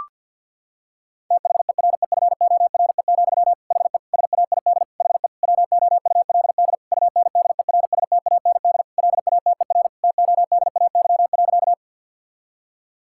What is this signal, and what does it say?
Morse code, heard as a keyed tone, about 50 wpm, 700 Hz. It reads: THEREFORE, HE SAID HE WOULD UNDERSTAND LATER TODAY.
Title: Sentences from Top 1000 Words - Review Part 5 - 50wpm